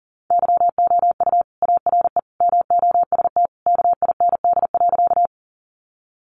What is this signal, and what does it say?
Morse code, heard as a keyed tone, about 40 wpm, 700 Hz. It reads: YOU ARE MOST KIND.
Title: Sentences from Top 200 Words 40wpm